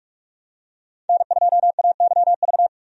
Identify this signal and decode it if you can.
Morse code, heard as a keyed tone, about 46 wpm, 700 Hz. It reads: N2AYV
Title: USA Call Signs - 5 Characters - 45wpm